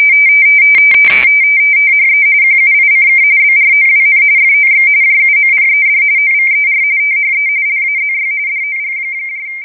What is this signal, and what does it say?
RTTY, about 45.45 baud. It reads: THE QUICK BROWN FOX JUMPED OVER THE LAZY DOG
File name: rtty / websdr recording 14086.0kHz